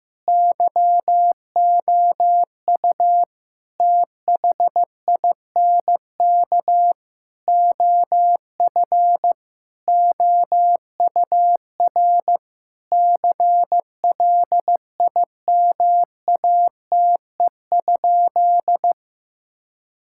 Morse code, heard as a keyed tone, about 15 wpm, 700 Hz. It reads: YOU THINK OF OUR CLIMATE?
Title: Sentences from Top 1600 Words 15wpm